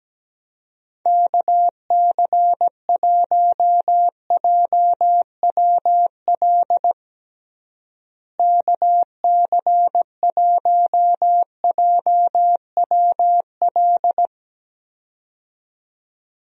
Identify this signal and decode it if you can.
Morse code, heard as a keyed tone, about 17 wpm, 700 Hz. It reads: KC1JWL KC1JWL
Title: USA Call Signs - 6 Characters - Repeated 2x - 17wpm